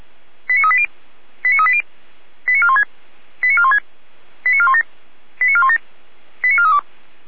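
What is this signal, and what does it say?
Five-tone selective call (ZVEI), heard as quick runs of five tones, tones of about 80 ms. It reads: 89380 89380 89416 89416 89417 89417 89432
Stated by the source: ZVEI five-tone selective call 89402 received by SDR.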